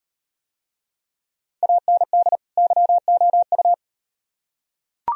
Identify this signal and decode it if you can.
Morse code, heard as a keyed tone, about 38 wpm, 700 Hz. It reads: AND YOU